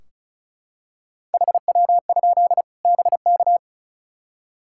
Morse code, heard as a keyed tone, about 36 wpm, 700 Hz. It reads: HW? BK
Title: K1USN SST QSO Practice 35wpm